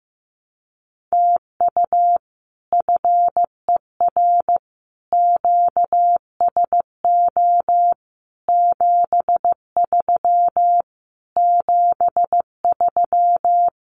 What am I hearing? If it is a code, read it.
Morse code, heard as a keyed tone, about 15 wpm, 700 Hz. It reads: TU FER QSO 73 73
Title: Common QSO Phrases Encore 15wpm